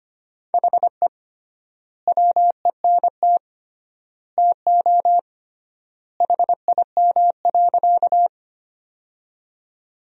Morse code, heard as a keyed tone, about 25 wpm, 700 Hz. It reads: HE WENT TO HIM.
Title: 3x Word-Spacing - Sentences from Top 200 Words - 25wpm